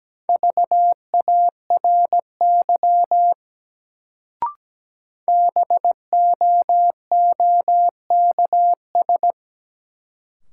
Morse code, heard as a keyed tone, about 17 wpm, 700 Hz. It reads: VARY BOOKS
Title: Sentences from Top 1200 Words - New Word Drill Alt - 17wpm